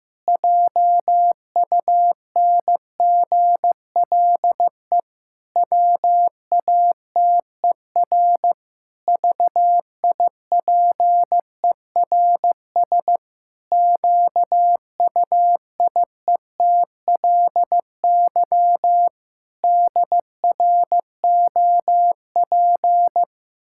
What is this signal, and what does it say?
Morse code, heard as a keyed tone, about 15 wpm, 700 Hz. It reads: JUNGLE WATER VIPERS QUIETLY DROP
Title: Pangram Challenge v2 - 15wpm